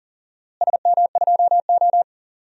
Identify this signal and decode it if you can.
Morse code, heard as a keyed tone, about 40 wpm, 700 Hz. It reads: SM2O
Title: DXCC-100 Call Signs - 4 Characters - Repeated 2x - 40wpm